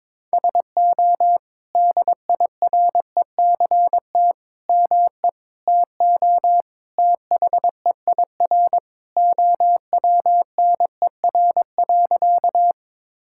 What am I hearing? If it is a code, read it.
Morse code, heard as a keyed tone, about 21 wpm, 700 Hz. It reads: SO DIRECT ME TO THEIR OWNER.
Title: Sentences from Top 1600 Words 22wpm